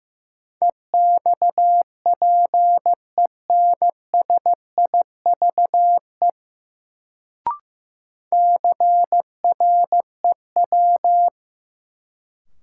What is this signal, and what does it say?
Morse code, heard as a keyed tone, about 15 wpm, 700 Hz. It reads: EXPENSIVE CREW
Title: Sentences from Top 1500 Words - New Word Drill Alt - 15wpm